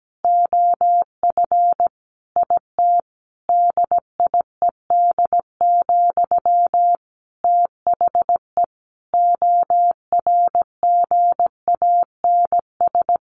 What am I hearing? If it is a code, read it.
Morse code, heard as a keyed tone, about 17 wpm, 700 Hz. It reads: OF IT DIED, THE ORGANS
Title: A Sound Of Thunder 17wpm